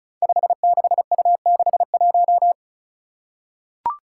Morse code, heard as a keyed tone, about 34 wpm, 700 Hz. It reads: 56U61
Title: Five Challenging Letter-Number Combinations 35wpm